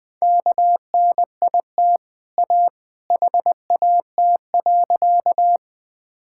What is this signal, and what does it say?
Morse code, heard as a keyed tone, about 20 wpm, 700 Hz. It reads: KNIT A HAT.
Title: Easy Sentences with Hints - 20wpm